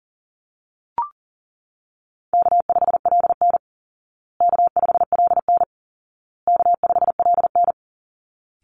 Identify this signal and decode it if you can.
Morse code, heard as a keyed tone, about 37 wpm, 700 Hz. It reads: K5LN K5LN K5LN
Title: Easy Call Signs Repeated 3x - 40wpm